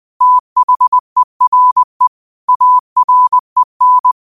Morse code, heard as a keyed tone, about 20 wpm, 1000 Hz. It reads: THERE AREN
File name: qotdfiles / 20wpm